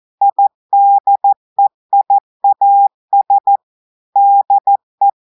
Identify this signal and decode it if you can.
Morse code, heard as a keyed tone, about 14 wpm, 800 Hz. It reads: IDEIAS DE
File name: audios / portugais / portugais texte en clair 2 vitesse 14 mots minutes